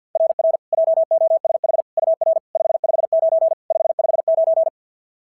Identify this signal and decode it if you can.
Morse code, heard as a keyed tone, about 52 wpm, 690 Hz. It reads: RR JOSH UR 559 559
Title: SOTA QSO Practice 50wpm